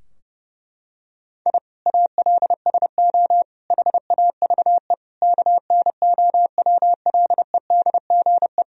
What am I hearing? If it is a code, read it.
Morse code, heard as a keyed tone, about 30 wpm, 700 Hz. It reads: I ALSO HAVE KNOWLEDGE
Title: Sentences from Top 1000 Words - Review Part 3 - 30wpm